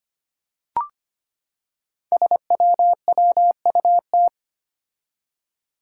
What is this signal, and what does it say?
Morse code, heard as a keyed tone, about 25 wpm, 700 Hz. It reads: SWWUT
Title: Five Challenging Letter-Number Combinations 25wpm